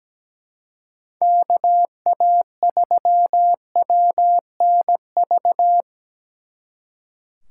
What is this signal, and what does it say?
Morse code, heard as a keyed tone, about 20 wpm, 700 Hz. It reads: KA3WNV
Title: Speed Racing - USA 2x3 Call Signs 17wpm